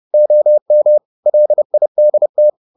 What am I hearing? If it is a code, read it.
Morse code, heard as a keyed tone, about 30 wpm, 600 Hz. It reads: OM LIDT
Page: Ferie 30 wpm